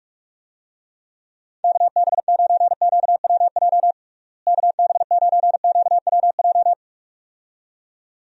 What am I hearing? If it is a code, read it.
Morse code, heard as a keyed tone, about 41 wpm, 700 Hz. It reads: KB9QWJ KB9QWJ
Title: USA Call Signs - 6 Characters - Repeated 2x - 45wpm